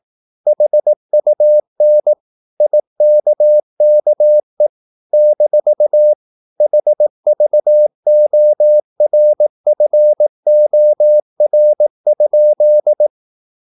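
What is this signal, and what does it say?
Morse code, heard as a keyed tone, about 18 wpm, 600 Hz. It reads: HUN IKKE - HVORFOR?
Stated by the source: Slow (QRS) Morse practice at 18 wpm, CW med Gnister.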